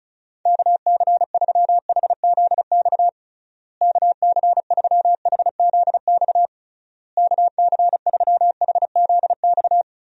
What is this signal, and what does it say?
Morse code, heard as a keyed tone, about 36 wpm, 700 Hz. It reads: KC3HZX KC3HZX KC3HZX
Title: USA Call Signs - 6 Characters - Repeated 3x - 35wpm